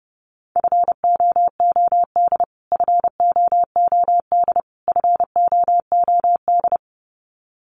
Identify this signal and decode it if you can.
Morse code, heard as a keyed tone, about 30 wpm, 700 Hz. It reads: FOOD FOOD FOOD
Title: Top 500 Words Repeated 30wpm